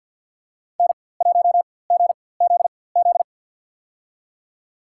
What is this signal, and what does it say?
Morse code, heard as a keyed tone, about 50 wpm, spaced as for 23 wpm, 700 Hz. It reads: N1GZZ
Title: USA Call Signs - 4x Character Spacing - 50wpm